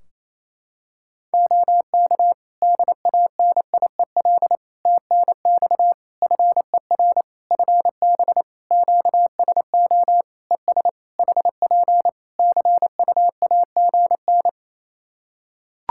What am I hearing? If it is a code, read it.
Morse code, heard as a keyed tone, about 28 wpm, 700 Hz. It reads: OK DANIEL TNX FER FB QSO ES HP CUAGN
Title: Practice QSO - Exchange 4 - Ending - 28wpm